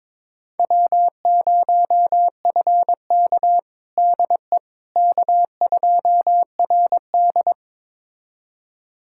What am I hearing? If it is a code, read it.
Morse code, heard as a keyed tone, about 22 wpm, 700 Hz. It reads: W0FK DE K2RD
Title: USA Callsign DE Callsign - 4 Characters - 22wpm